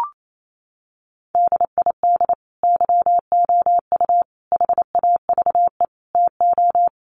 Morse code, heard as a keyed tone, about 28 wpm, 700 Hz. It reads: DID YOU HAVE TO
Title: Sentences from Top 1500 Words 28wpm